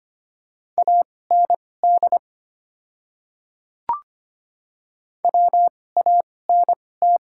Morse code, heard as a keyed tone, about 25 wpm, spaced as for 18 wpm, 700 Hz. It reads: AND WANT
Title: 2x Character Spacing - Top 200 Words - 25wpm